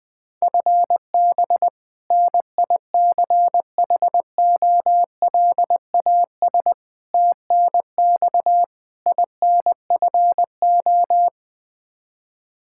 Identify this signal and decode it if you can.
Morse code, heard as a keyed tone, about 20 wpm, 700 Hz. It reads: FB NICHOLAS TNX INFO
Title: K1USN SST QSO Practice 20wpm